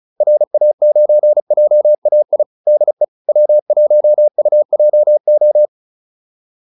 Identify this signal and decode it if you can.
Morse code, heard as a keyed tone, about 37 wpm, 600 Hz. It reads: RA9JAI DE W1UJO